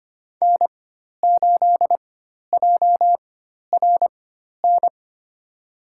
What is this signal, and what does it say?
Morse code, heard as a keyed tone, about 25 wpm, spaced as for 12 wpm, 700 Hz. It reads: N8JRN
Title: USA Call Signs - 5 Characters - 4x Character Spacing - 25wpm